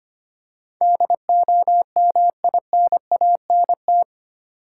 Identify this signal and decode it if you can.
Morse code, heard as a keyed tone, about 25 wpm, 700 Hz. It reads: DOMINANT
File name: Eight Letter Words - 25wpm